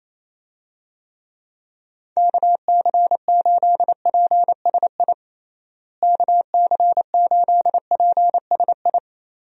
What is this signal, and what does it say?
Morse code, heard as a keyed tone, about 28 wpm, 700 Hz. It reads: KC8PSI KC8PSI
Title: USA Call Signs - 6 Characters - Repeated 3x - 28wpm